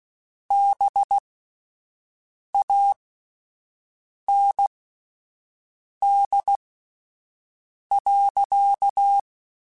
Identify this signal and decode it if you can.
Morse code, heard as a keyed tone, about 16 wpm, spaced as for 5 wpm, 800 Hz. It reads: BAND.
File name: QSOs Spaced